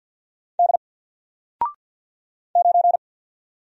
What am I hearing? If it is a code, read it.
Morse code, heard as a keyed tone, about 50 wpm, 700 Hz. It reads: D 9
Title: Single Character (Letter:Number:Punctuation:Prosign) 50wpm